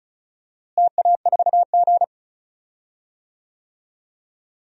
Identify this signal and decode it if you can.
Morse code, heard as a keyed tone, about 34 wpm, 700 Hz. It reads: TA4G
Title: DXCC-100 Call Signs - 4 Characters - Repeated 2x - 35wpm